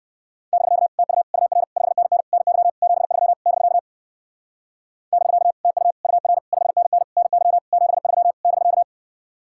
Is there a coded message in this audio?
Yes: Morse code from 0.5 to 8.8 s, heard as a keyed tone, about 58 wpm, 700 Hz. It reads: <BK> TU UR 5NN NY 73 <BK> <BK> TU UR 5NN NY 73 <BK>